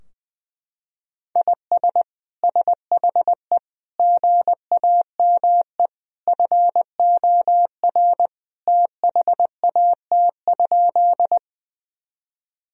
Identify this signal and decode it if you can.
Morse code, heard as a keyed tone, about 20 wpm, 700 Hz. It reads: IS SHE GAME FOR THAT?
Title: Sentences from Top 1000 Words - Review Part 3 - 20wpm